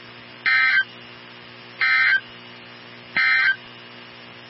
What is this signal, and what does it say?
Emergency Alert System (SAME) tones, heard as harsh buzzing bursts. It is heard from 0.3 to 3.6 s.
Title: NOAA Weather Radio 162.550